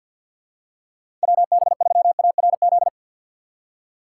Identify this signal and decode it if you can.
Morse code, heard as a keyed tone, about 49 wpm, 700 Hz. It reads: WB3ARZ
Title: USA Call Signs - 6 Characters - 4x Character Spacing - 50wpm